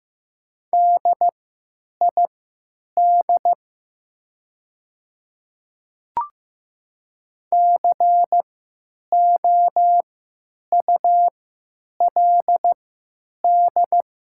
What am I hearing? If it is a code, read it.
Morse code, heard as a keyed tone, about 15 wpm, spaced as for 9 wpm, 700 Hz. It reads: DID COULD
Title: 3x Character Spacing - Top 100 Words - 15wpm